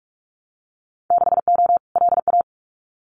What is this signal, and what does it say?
Morse code, heard as a keyed tone, about 44 wpm, 700 Hz. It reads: 6O LA
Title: Winter Field Day Exchanges Encore 45wpm